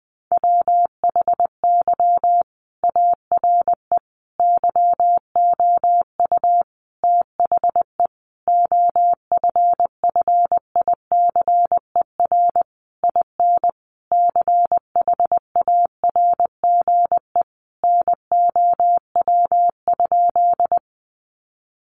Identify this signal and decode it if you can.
Morse code, heard as a keyed tone, about 20 wpm, 700 Hz. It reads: WHY ARE YOU THE OFFICER IN CHARGE NOW?
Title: Sentences from Top 1200 Words 20wpm